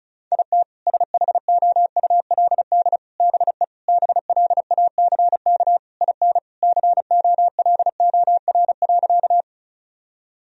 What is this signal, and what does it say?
Morse code, heard as a keyed tone, about 36 wpm, 700 Hz. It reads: IT SHOULD BE BLACK IN COLOR.
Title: Sentences from Top 1000 Words - Review Part 2 - 35wpm